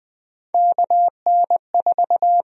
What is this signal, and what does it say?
Morse code, heard as a keyed tone, about 20 wpm, 700 Hz. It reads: KN4
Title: USA Call Sign Prefixes - Repeated 2x - 20wpm